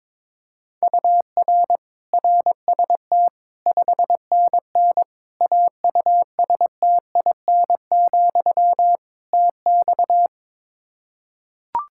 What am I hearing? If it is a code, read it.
Morse code, heard as a keyed tone, about 22 wpm, 700 Hz. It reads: UR RST 5NN AUSTIN, TX
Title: Practice QSO - Exchange 1 - Beginning - 22wpm